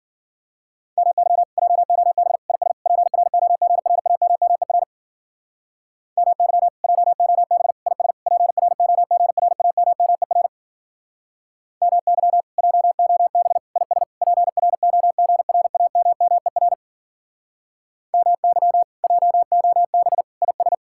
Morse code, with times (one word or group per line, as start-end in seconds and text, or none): 1.0-1.4 MY
1.6-2.4 JOB
2.5-2.7 IS
2.9-4.8 PROGRAMMER
6.2-6.7 MY
6.8-7.7 JOB
7.9-8.1 IS
8.3-10.5 PROGRAMMER
11.8-12.4 MY
12.6-13.6 JOB
13.8-14.0 IS
14.2-16.7 PROGRAMMER
18.1-18.8 MY
19.0-20.2 JOB
20.4-20.7 IS